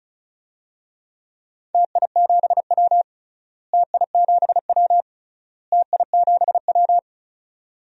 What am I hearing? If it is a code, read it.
Morse code, heard as a keyed tone, about 36 wpm, 700 Hz. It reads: TI7W TI7W TI7W
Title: DXCC-100 Call Signs - 4 Characters - Repeated 3x - 35wpm